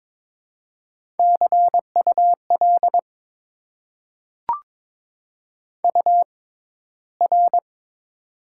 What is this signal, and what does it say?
Morse code, heard as a keyed tone, about 22 wpm, spaced as for 13 wpm, 700 Hz. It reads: CUL UR
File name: Top 90 QSO Elements - 6x Character Spacing - 22wpm